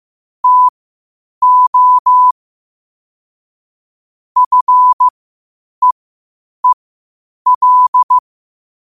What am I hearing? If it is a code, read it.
Morse code, heard as a keyed tone, about 15 wpm, spaced as for 8 wpm, 1000 Hz. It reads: TO FEEL
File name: qotdfiles / 8wpm